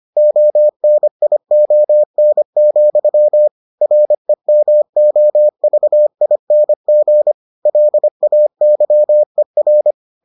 Morse code, heard as a keyed tone, about 25 wpm, 600 Hz. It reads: ONION, REMOVING LAYER